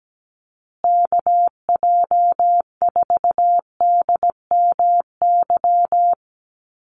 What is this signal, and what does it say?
Morse code, heard as a keyed tone, about 17 wpm, 700 Hz. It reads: KJ4DMY
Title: USA Call Signs Repeated 2x 17wpm